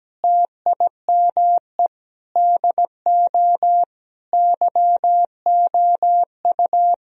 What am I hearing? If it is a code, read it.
Morse code, heard as a keyed tone, about 17 wpm, 700 Hz. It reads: TIME DO YOU
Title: Sentences from Top 1000 Words - Review Part 4 - 17wpm